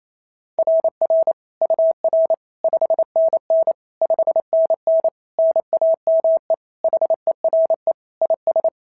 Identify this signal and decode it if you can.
Morse code, heard as a keyed tone, about 25 wpm, 650 Hz. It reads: RR UR 5NN 5NN NAME HERE IS
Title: SOTA QSO Practice 28wpm